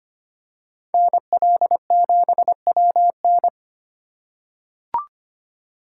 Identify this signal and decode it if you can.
Morse code, heard as a keyed tone, about 25 wpm, 700 Hz. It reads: NL7WN
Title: USA Call Signs - 5 Characters - 8x Character Spacing - 25wpm